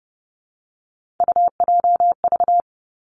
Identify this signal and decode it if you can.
Morse code, heard as a keyed tone, about 30 wpm, 700 Hz. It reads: UJV
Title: Three Challenging Letter-Number Combinations 30wpm